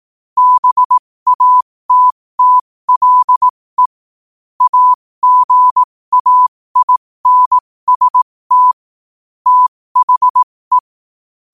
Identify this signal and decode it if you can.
Morse code, heard as a keyed tone, about 18 wpm, spaced as for 15 wpm, 1000 Hz. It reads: BATTLE AGAINST THE
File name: qotdfiles / 15wpm